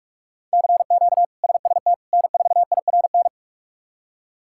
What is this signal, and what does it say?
Morse code, heard as a keyed tone, about 47 wpm, 700 Hz. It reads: CQ SST N4IRN
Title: K1USN SST QSO Practice - 45wpm